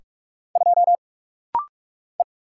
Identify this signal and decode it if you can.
Morse code, heard as a keyed tone, about 44 wpm, 700 Hz. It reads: 2 E
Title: Single Letter-Number - Rapid-Fire 45wpm